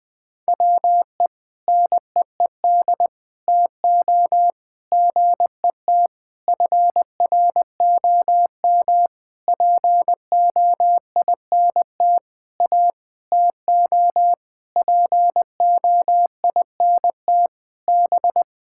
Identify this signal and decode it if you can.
Morse code, heard as a keyed tone, about 20 wpm, 700 Hz. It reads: WE NEED TO GET FROM POINT A TO POINT B
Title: Sentences with 6-Word Phrases - Spoken Beginning - 20wpm